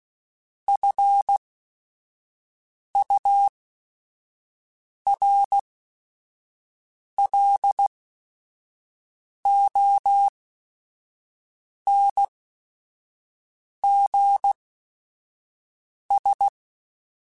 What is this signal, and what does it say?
Morse code, heard as a keyed tone, about 16 wpm, spaced as for 5 wpm, 800 Hz. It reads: FURLONGS
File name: F Runs / F Runs mp3s / F Runs @5